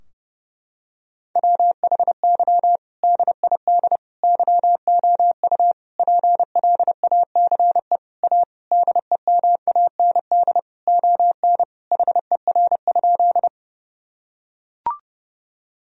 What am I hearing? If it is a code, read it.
Morse code, heard as a keyed tone, about 31 wpm, 700 Hz. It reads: WHY DID YOU PLACE A DEMAND ON HER?
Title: Sentences from Top 1000 Words - Review Part 4 - 30wpm